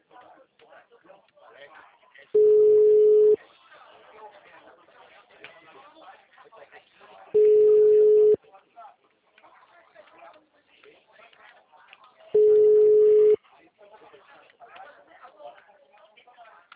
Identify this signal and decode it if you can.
Ringback tone (Europe, 425 Hz), with speech.